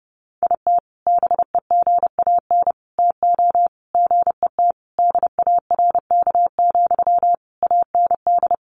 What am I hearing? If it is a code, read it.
Morse code, heard as a keyed tone, about 30 wpm, 700 Hz. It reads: IT BEGAN TO GET DARK, AND
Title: Sentences from Top 1400 Words 30wpm